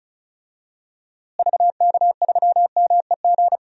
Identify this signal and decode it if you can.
Morse code, heard as a keyed tone, about 36 wpm, 700 Hz. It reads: VK3MEG
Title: DXCC-100 Call Signs - 6 Characters - Repeated 3x - 35wpm